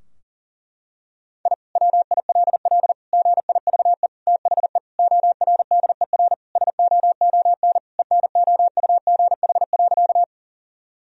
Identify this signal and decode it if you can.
Morse code, heard as a keyed tone, about 37 wpm, 700 Hz. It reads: I WILL GIVE THE ORDER SOON ENOUGH.
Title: Sentences from Top 1000 Words - Review Part 2 - 40wpm